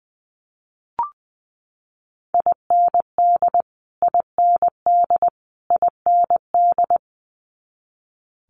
Morse code, heard as a keyed tone, about 20 wpm, 700 Hz. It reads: IND IND IND
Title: Multi-letter Phonemes Repeated 3x 20wpm